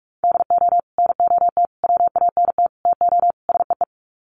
Morse code, heard as a keyed tone, about 44 wpm, 700 Hz. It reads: DO NOT WANT TO SEE
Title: Sets of 5 Words 45wpm